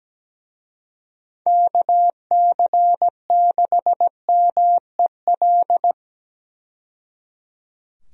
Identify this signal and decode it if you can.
Morse code, heard as a keyed tone, about 17 wpm, 700 Hz. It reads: KC6MEL
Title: USA Call Signs - 6 Characters - 17wpm